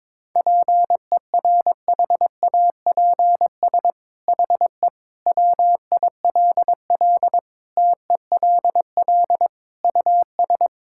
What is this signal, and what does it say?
Morse code, heard as a keyed tone, about 22 wpm, 700 Hz. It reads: PERHAPS HE WILL TELL US
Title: Sentences from Top 1600 Words 22wpm